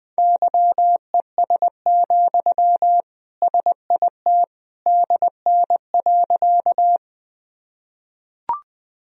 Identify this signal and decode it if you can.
Morse code, heard as a keyed tone, about 20 wpm, 700 Hz. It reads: YES, SIT DN.
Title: Sentences from Top 500 Words and Abbreviations 20wpm